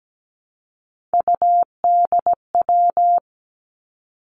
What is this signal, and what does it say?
Morse code, heard as a keyed tone, about 17 wpm, 700 Hz. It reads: UDW
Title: Three Challenging Letter-Number Combinations 17wpm